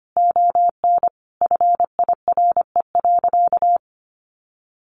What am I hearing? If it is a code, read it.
Morse code, heard as a keyed tone, about 25 wpm, 700 Hz. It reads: ON FIRE.
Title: Sentences from Top 1100 Words 25wpm